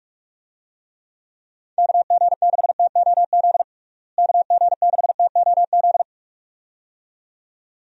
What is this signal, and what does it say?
Morse code, heard as a keyed tone, about 46 wpm, 700 Hz. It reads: KG6TOZ KG6TOZ
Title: USA Call Signs - 6 Characters - Repeated 2x - 45wpm